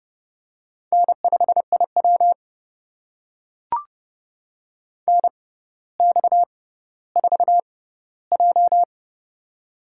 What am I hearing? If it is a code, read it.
Morse code, heard as a keyed tone, about 30 wpm, spaced as for 15 wpm, 700 Hz. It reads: N5IW NX4J
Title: USA Call Signs - 4 Characters - 6x Character Spacing - 30wpm